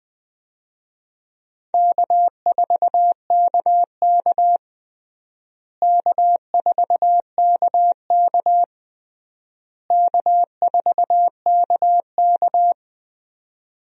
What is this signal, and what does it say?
Morse code, heard as a keyed tone, about 20 wpm, 700 Hz. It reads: K4KK K4KK K4KK
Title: USA Call Signs - 4 Characters - Repeated 3x - 20wpm